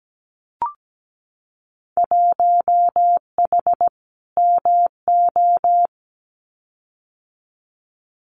Morse code, heard as a keyed tone, about 17 wpm, 700 Hz. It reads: 1H MO
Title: Winter Field Day Exchanges 17wpm